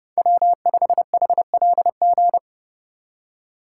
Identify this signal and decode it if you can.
Morse code, heard as a keyed tone, about 29 wpm, 700 Hz. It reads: W5HLG